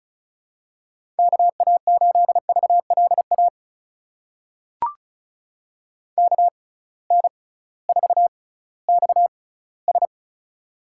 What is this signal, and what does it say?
Morse code, heard as a keyed tone, about 36 wpm, spaced as for 19 wpm, 700 Hz. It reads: KA8VLA KN4XS
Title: USA Call Signs - 6x Character Spacing - 35wpm